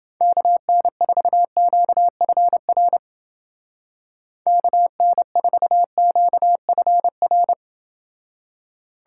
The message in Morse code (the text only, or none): KN4QFR KN4QFR